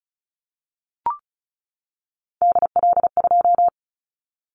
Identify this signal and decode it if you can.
Morse code, heard as a keyed tone, about 34 wpm, 700 Hz. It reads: DL2
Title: Callsign Prefixes for Top 25 DXCC - 35wpm